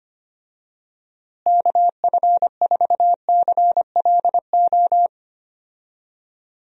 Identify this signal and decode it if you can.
Morse code, heard as a keyed tone, about 25 wpm, 700 Hz. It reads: KF4CLO
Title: USA Call Signs - 6 Characters - Repeated 3x - 25wpm